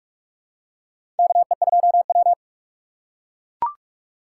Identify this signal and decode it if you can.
Morse code, heard as a keyed tone, about 45 wpm, 700 Hz. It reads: KE2W
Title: USA Call Signs - 4 Characters - 2x Character Spacing - 45wpm